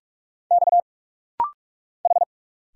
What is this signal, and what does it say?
Morse code, heard as a keyed tone, about 44 wpm, 700 Hz. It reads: X H
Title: Single Letter-Number - Rapid-Fire 45wpm